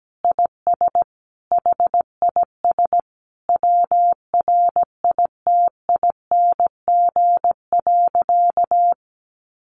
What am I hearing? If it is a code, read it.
Morse code, heard as a keyed tone, about 17 wpm, 700 Hz. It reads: IS HIS WRITING.
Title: Sentences from Top 800 Words 17wpm